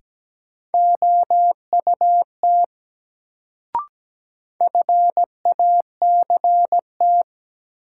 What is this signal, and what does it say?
Morse code, heard as a keyed tone, about 17 wpm, 700 Hz. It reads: OUT FACT
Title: Sets of 2 Words - New Word Drill - 17wpm